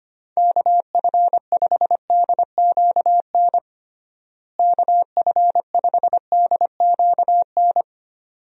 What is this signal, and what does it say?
Morse code, heard as a keyed tone, about 25 wpm, 700 Hz. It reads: KF5DQN KF5DQN
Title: USA Call Signs - 6 Characters - Repeated 2x - 25wpm